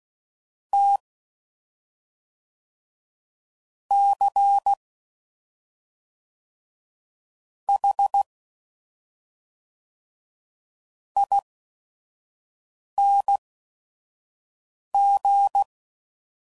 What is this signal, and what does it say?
Morse code, heard as a keyed tone, about 16 wpm, spaced as for 5 wpm, 800 Hz. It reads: T C H ING
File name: Suffixes / Special ING's / SlowToFaster ING's